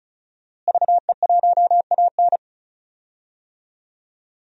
Morse code, heard as a keyed tone, about 34 wpm, 700 Hz. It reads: VE1AN
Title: DXCC-100 Call Signs - 5 Characters - Repeated 2x - 35wpm